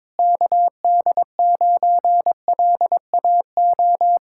Morse code, heard as a keyed tone, about 21 wpm, 700 Hz. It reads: KD9LAO
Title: Speed-Racing Common QSO Phrases - 22wpm